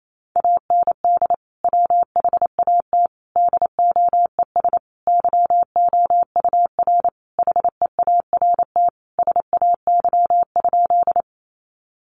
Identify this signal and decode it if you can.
Morse code, heard as a keyed tone, about 28 wpm, 700 Hz. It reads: AND WHAT DOES YOUR HEART SAY?
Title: Sentences from Top 1200 Words 28wpm